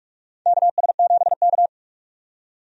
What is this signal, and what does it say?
Morse code, heard as a keyed tone, about 46 wpm, 700 Hz. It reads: KS7K
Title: USA Call Signs - 4 Characters - 45wpm